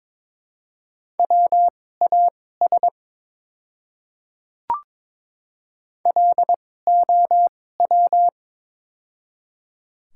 Morse code, heard as a keyed tone, about 21 wpm, spaced as for 16 wpm, 700 Hz. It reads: WAS LOW
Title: 2x Character Spacing - Top 200 Words - 22wpm